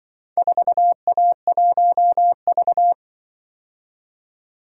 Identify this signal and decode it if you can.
Morse code, heard as a keyed tone, about 24 wpm, 700 Hz. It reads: 4A1V
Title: Speed-Racing - Four Letter-Number Combinations - 20wpm